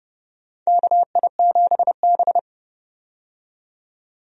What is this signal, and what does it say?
Morse code, heard as a keyed tone, about 30 wpm, 700 Hz. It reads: KI7B
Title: USA Call Signs - 4 Characters - 30wpm